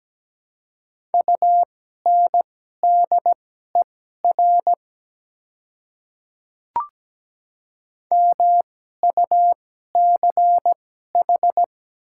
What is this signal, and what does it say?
Morse code, heard as a keyed tone, about 17 wpm, spaced as for 12 wpm, 700 Hz. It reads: UNDER MUCH
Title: 2x Character Spacing - Top 300 Words - 17wpm